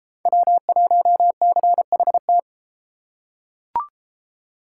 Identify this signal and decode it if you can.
Morse code, heard as a keyed tone, about 30 wpm, 700 Hz. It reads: W1CHT